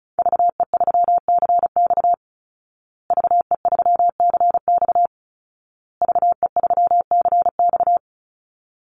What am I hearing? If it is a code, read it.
Morse code, heard as a keyed tone, about 34 wpm, 700 Hz. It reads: VE3CX VE3CX VE3CX
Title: Call Signs Repeated 2x - 35wpm